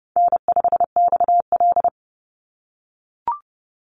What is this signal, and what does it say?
Morse code, heard as a keyed tone, about 30 wpm, 700 Hz. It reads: N5XL
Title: NA SOTA Chaser Callsigns 30wpm